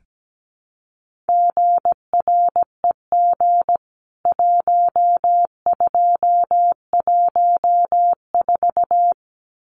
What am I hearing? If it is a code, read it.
Morse code, heard as a keyed tone, about 17 wpm, 700 Hz. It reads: GREG 1214
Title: CWT Exchange 17wpm